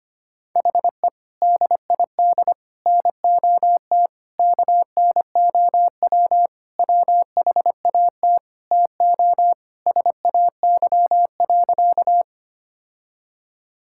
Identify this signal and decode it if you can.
Morse code, heard as a keyed tone, about 25 wpm, 700 Hz. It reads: HE DID NOT KNOW WHAT TO SAY.
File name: practice / 25wpm / Sentences with 3-Word Phrases - 25wpm